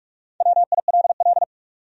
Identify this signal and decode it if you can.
Morse code, heard as a keyed tone, about 63 wpm, spaced as for 54 wpm, 700 Hz. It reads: WILL
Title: Speed-Racing Top 200 Words 45wpm